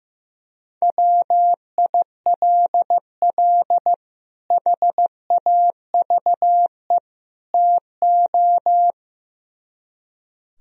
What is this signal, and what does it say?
Morse code, heard as a keyed tone, about 15 wpm, 700 Hz. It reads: WILL HAVE TO
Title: Speed-Racing - Sets of 3 Words 15wpm